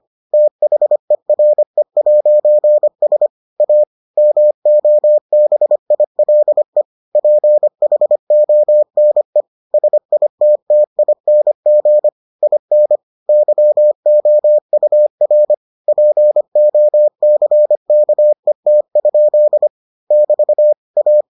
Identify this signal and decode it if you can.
Morse code, heard as a keyed tone, about 25 wpm, 600 Hz. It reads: THERE'S A MOBILE PHONE SITTING IN YOUR POCKET? <BT> A